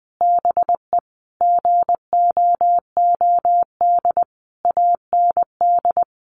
Morse code, heard as a keyed tone, about 19 wpm, 700 Hz. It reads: BE GOOD AND
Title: Sentences from Top 1100 Words 20wpm